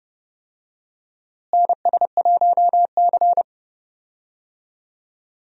Morse code, heard as a keyed tone, about 30 wpm, 700 Hz. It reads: NS1C
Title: USA Call Signs - 4 Characters - Repeated 3x - 30wpm